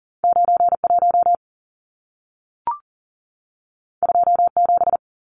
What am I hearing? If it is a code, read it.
Morse code, heard as a keyed tone, about 40 wpm, 700 Hz. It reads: 91 27
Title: Numbers - 2-Digit 40wpm